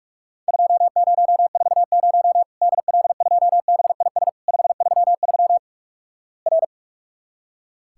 Morse code, heard as a keyed tone, about 44 wpm, 700 Hz. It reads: 2040 DL2BIS 533 R
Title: Worked All Europe DX Contest - QTC Exchange - 45wpm